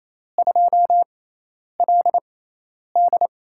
Morse code, heard as a keyed tone, about 27 wpm, 700 Hz. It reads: 2 L D
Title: USA Call Signs - 4 Characters - 6x Character Spacing - 28wpm